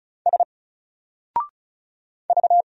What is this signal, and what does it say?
Morse code, heard as a keyed tone, about 36 wpm, 700 Hz. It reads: S V
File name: Single-Letter-Number-35wpm